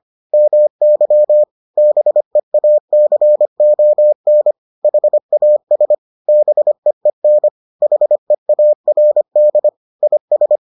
Morse code, heard as a keyed tone, about 25 wpm, 600 Hz. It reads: MY BEACON HAS BEEN HEARD IS